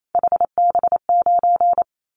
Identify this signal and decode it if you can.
Morse code, heard as a keyed tone, about 28 wpm, 700 Hz. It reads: HB9
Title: Callsign Prefixes for Top 25 DXCC - 28wpm